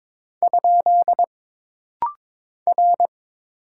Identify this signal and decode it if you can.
Morse code, heard as a keyed tone, about 22 wpm, 700 Hz. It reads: ? R
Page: This rapid-fire timing is geared to push your ICR (Instant Character Recognition) skills.